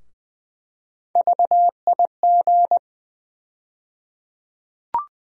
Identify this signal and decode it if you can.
Morse code, heard as a keyed tone, about 20 wpm, 700 Hz. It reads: VIG
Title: USA Call Sign Postfixes - 3 Letters 20wpm